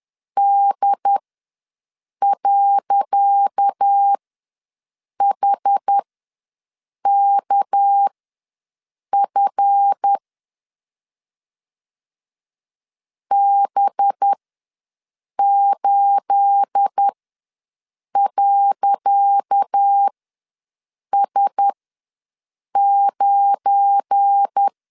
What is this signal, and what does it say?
Morse code, heard as a keyed tone, about 10 wpm, spaced as for 6 wpm, 800 Hz. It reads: D.HKF B8.S9
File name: LEZIONI / 10 SIGNS / 10C DOT ws 10wpm 4213